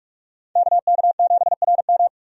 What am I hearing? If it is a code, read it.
Morse code, heard as a keyed tone, about 46 wpm, 700 Hz. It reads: KK7RM
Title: USA Call Signs - 5 Characters - 45wpm